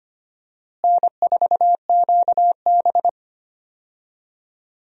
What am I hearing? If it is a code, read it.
Morse code, heard as a keyed tone, about 25 wpm, 700 Hz. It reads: N4QB
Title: USA Call Signs - 4 Characters - Repeated 2x - 25wpm